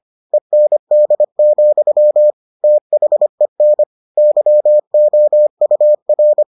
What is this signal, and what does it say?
Morse code, heard as a keyed tone, about 25 wpm, 600 Hz. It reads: END, THEN YOUR